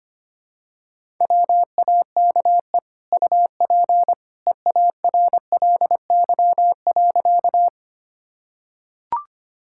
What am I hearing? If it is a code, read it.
Morse code, heard as a keyed tone, about 25 wpm, 700 Hz. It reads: WAKE UP EARLY.
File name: Easy Sentences with Hints - 25wpm